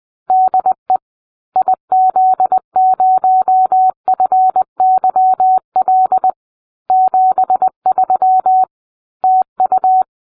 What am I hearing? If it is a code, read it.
Morse code, heard as a keyed tone, about 20 wpm, 750 Hz. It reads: DE IZ0FYL 73 TU